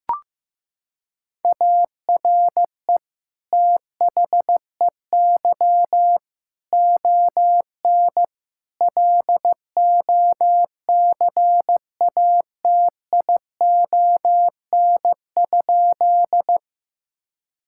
Morse code, heard as a keyed tone, about 15 wpm, 700 Hz. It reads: ARE THEY ON LOCATION?
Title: Sentences from Top 1000 Words - Review Part 6 - 15wpm